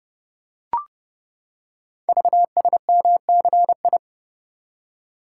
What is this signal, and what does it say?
Morse code, heard as a keyed tone, about 28 wpm, 700 Hz. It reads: VSMCI
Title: Code Groups - 5 Letter:Numbers - 30wpm